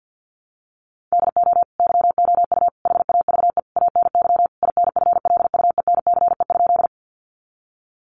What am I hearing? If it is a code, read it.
Morse code, heard as a keyed tone, about 50 wpm, 700 Hz. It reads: DO YOU HAVE ANY INFLUENCE?
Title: Sentences from Top 1100 Words 50wpm